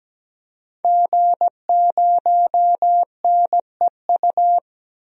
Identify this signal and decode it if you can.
Morse code, heard as a keyed tone, about 17 wpm, 700 Hz. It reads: G0NEU
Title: DXCC-100 Call Signs - 5 Characters - 17wpm